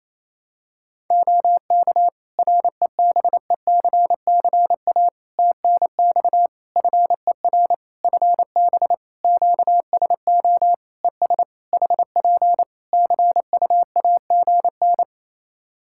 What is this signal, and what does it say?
Morse code, heard as a keyed tone, about 28 wpm, 700 Hz. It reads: OK REBECCA TNX FER FB QSO ES HP CUAGN
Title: Practice QSO - Exchange 4 - Ending - 28wpm